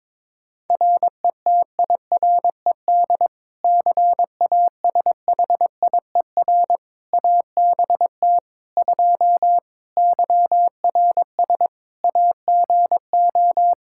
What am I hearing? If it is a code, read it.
Morse code, heard as a keyed tone, about 21 wpm, 700 Hz. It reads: RETIRED CASHIER ABT 2 YRS AGO
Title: K1USN SST QSO Practice 22wpm